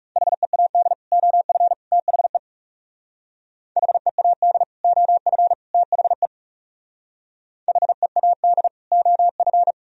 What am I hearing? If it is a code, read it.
Morse code, heard as a keyed tone, about 37 wpm, 700 Hz. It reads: HEAD OF THE HEAD OF THE HEAD OF
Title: Speed-Racing - Sets of 3 Words 30wpm